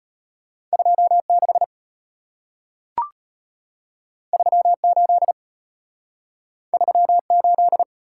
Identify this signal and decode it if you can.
Morse code, heard as a keyed tone, about 34 wpm, 700 Hz. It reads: 26 38 38